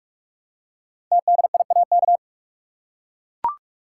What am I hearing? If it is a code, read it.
Morse code, heard as a keyed tone, about 44 wpm, 700 Hz. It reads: TDIAK
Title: Code Groups - 5 Letters - 45wpm